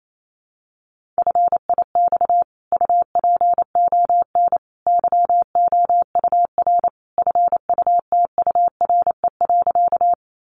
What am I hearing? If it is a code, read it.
Morse code, heard as a keyed tone, about 28 wpm, 700 Hz. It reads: FIX UPON YOUR FUTURE.
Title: Sentences from Top 1400 Words 28wpm